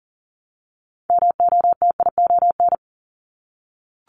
Morse code, heard as a keyed tone, about 40 wpm, 700 Hz. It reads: MOTION
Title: Multi-letter Phonemes Repeated 3x 40wpm